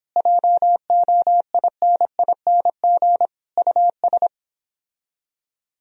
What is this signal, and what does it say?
Morse code, heard as a keyed tone, about 26 wpm, 700 Hz. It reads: JOINING US
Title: Speed-Racing - Sets of 5 Words 22wpm